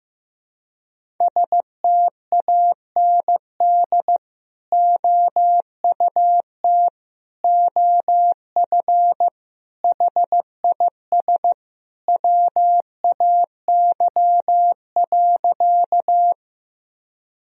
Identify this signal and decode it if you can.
Morse code, heard as a keyed tone, about 16 wpm, 700 Hz. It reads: STAND OUT OF HIS WAY.
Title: Sentences from Top 1000 Words - Review Part 1 - 15wpm